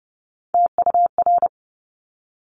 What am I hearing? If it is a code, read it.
Morse code, heard as a keyed tone, about 30 wpm, 700 Hz. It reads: TUR
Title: Top Three Letter Combinations in Words 30wpm